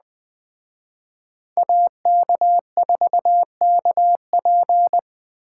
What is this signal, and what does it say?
Morse code, heard as a keyed tone, about 20 wpm, 700 Hz. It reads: AK4KP
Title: Speed Racing - USA 2x2 Call Signs 17wpm